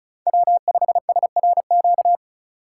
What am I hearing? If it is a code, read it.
Morse code, heard as a keyed tone, about 36 wpm, 700 Hz. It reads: W5SRQ
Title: USA Call Signs - 5 Characters - 6x Character Spacing - 35wpm